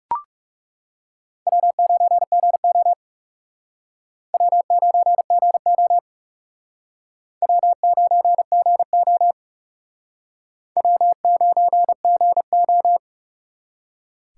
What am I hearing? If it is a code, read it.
Morse code, heard as a keyed tone, about 36 wpm, 700 Hz. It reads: W9GO W9GO W9GO W9GO
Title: Speed Racing - USA 1x2 Call Signs 30wpm